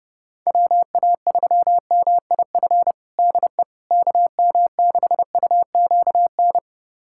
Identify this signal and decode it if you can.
Morse code, heard as a keyed tone, about 30 wpm, 700 Hz. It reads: WA3MIF DE KM6UQN
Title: USA Callsign DE Callsign - 6 Characters - 30wpm